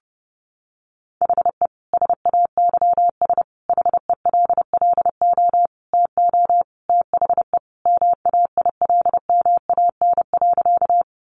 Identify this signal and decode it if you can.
Morse code, heard as a keyed tone, about 30 wpm, 700 Hz. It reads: HE SAYS HELLO TO THE MAILMAN.
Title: Easy Phrases 30wpm